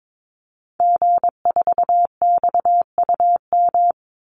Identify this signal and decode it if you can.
Morse code, heard as a keyed tone, about 22 wpm, 700 Hz. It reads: G4XUM
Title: Call Signs - 22wpm